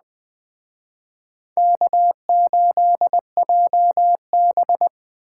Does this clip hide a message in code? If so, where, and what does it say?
Yes: Morse code from 1.6 to 4.9 s, heard as a keyed tone, about 20 wpm, 700 Hz. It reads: K8JB